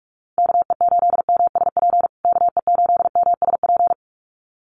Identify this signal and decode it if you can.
Morse code, heard as a keyed tone, about 44 wpm, 700 Hz. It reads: KE8MSP KE8MSP
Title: USA Call Signs Repeated 2x 45wpm